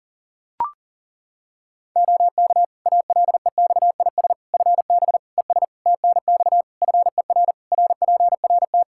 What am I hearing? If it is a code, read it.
Morse code, heard as a keyed tone, about 40 wpm, 700 Hz. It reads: OK ALEXIS FB ES TNX FER RPRT
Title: Practice QSO - Exchange 2 - Info - 40wpm